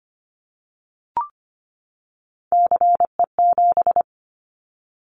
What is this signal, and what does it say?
Morse code, heard as a keyed tone, about 25 wpm, 700 Hz. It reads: CE7
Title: Callsign Prefixes for Top 100 DXCC - 25wpm